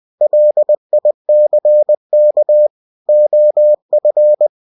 Morse code, heard as a keyed tone, about 20 wpm, 600 Hz. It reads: LICK OF
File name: cw-books / winnie-the-pooh / winnie-20wpm-page-0030